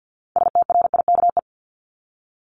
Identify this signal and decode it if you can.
Morse code, heard as a keyed tone, about 50 wpm, 700 Hz. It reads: STRIKE
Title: Top 1501-1750 Words 50wpm